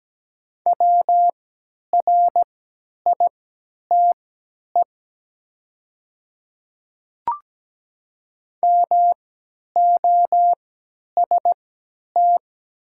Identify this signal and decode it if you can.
Morse code, heard as a keyed tone, about 17 wpm, spaced as for 10 wpm, 700 Hz. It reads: WRITE MOST
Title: 3x Character Spacing - Top 100 Words - 17wpm